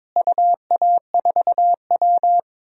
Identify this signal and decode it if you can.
Morse code, heard as a keyed tone, about 21 wpm, 700 Hz. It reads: UA4W
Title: DXCC-100 Call Signs - 4 Characters - Repeated 3x - 22wpm